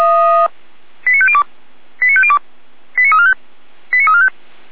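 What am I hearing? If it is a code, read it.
Five-tone selective call (ZVEI), heard as quick runs of five tones, tones of about 70 ms. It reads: 89502 89502 89345 89345
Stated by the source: ZVEI five-tone call 89138 received by SDR, archived by Feuerwehr Orlamünde (fire brigade).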